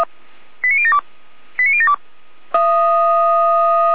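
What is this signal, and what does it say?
Five-tone selective call (ZVEI), heard as quick runs of five tones, tones of about 70 ms. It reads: 89072 89072
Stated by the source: ZVEI five-tone selective call received by SDR, from the Feuerwehr Orlamünde fire brigade audio server.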